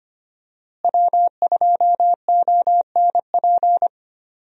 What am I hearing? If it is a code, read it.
Morse code, heard as a keyed tone, about 25 wpm, 700 Hz. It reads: W2ONP
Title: USA Call Signs - 5 Characters - Repeated 3x - 25wpm